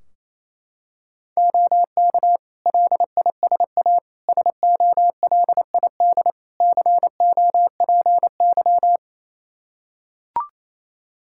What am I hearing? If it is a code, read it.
Morse code, heard as a keyed tone, about 28 wpm, 700 Hz. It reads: OK LISA SOLID COPY
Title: Practice QSO - Exchange 3 - More Info - 28wpm